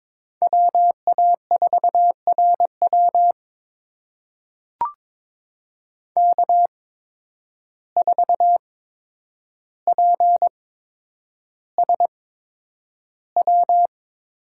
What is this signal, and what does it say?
Morse code, heard as a keyed tone, about 22 wpm, spaced as for 9 wpm, 700 Hz. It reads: WA4RW K4PSW
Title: USA Call Signs - 5 Characters - 8x Character Spacing - 22wpm